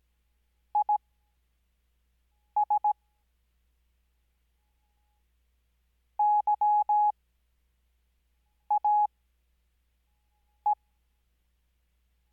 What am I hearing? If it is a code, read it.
Morse code, heard as a keyed tone, about 17 wpm, spaced as for 5 wpm, 840 Hz. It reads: IS YAE
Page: Farnsworth Spacing: 15WPM Character speed Spaced at 5WPM.